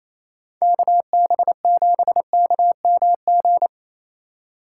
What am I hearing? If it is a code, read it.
Morse code, heard as a keyed tone, about 28 wpm, 700 Hz. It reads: KB7KMG
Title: USA Call Signs - 6 Characters - 2x Character Spacing - 28wpm